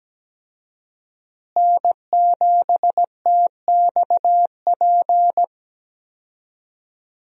Morse code, heard as a keyed tone, about 17 wpm, 700 Hz. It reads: N7TXP
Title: USA Call Signs - 5 Characters - Repeated 2x - 17wpm